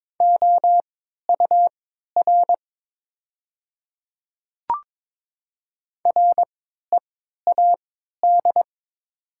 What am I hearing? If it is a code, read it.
Morse code, heard as a keyed tone, about 21 wpm, spaced as for 12 wpm, 700 Hz. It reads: OUR READ
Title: 3x Character Spacing - Top 200 Words - 22wpm